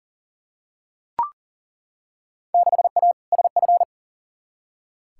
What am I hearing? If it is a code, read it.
Morse code, heard as a keyed tone, about 40 wpm, 700 Hz. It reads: 6A SF
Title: Field Day Exchanges - 40wpm